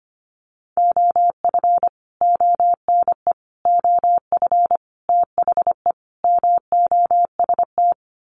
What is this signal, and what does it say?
Morse code, heard as a keyed tone, about 25 wpm, 700 Hz. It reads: OF ONE OF THE MOST
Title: Sets of 5 Words Encore2 25wpm